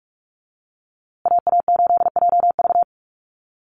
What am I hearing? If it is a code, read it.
Morse code, heard as a keyed tone, about 44 wpm, 700 Hz. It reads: AA8JV
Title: USA Call Signs Repeated 2x 45wpm